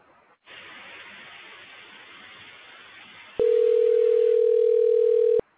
Ringback tone (North America), 440 + 480 Hz.